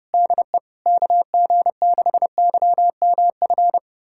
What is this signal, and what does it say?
Morse code, heard as a keyed tone, about 30 wpm, 690 Hz. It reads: DE KG6YMF
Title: USA Callsign DE Callsign 30wpm